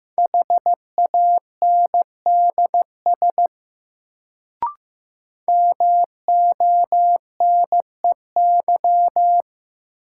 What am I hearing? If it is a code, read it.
Morse code, heard as a keyed tone, about 15 wpm, 700 Hz. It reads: HANDS MONEY
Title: Sets of 4 Words - New Word Drill - 15wpm